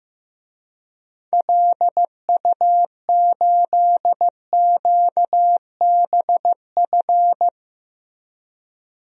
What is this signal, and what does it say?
Morse code, heard as a keyed tone, about 15 wpm, 700 Hz. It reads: LU8QBF
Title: DXCC-100 Call Signs - 6 Characters - Repeated 2x - 15wpm